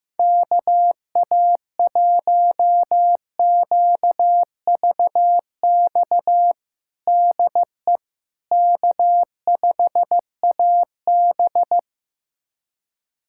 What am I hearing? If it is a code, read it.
Morse code, heard as a keyed tone, about 15 wpm, 700 Hz. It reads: KA1QVX DE K5AB
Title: USA Callsign DE Callsign 15wpm